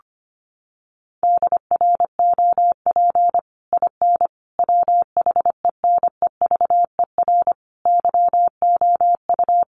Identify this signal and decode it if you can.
Morse code, heard as a keyed tone, about 25 wpm, 700 Hz. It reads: DROP IN WHENEVER YOU
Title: Sentences from Top 1400 Words 25wpm